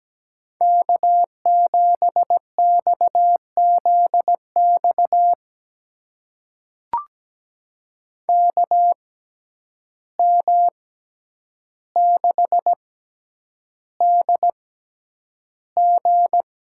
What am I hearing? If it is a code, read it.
Morse code, heard as a keyed tone, about 17 wpm, spaced as for 9 wpm, 700 Hz. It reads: K7XZX KM6DG
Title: USA Call Signs - 6x Character Spacing - 17wpm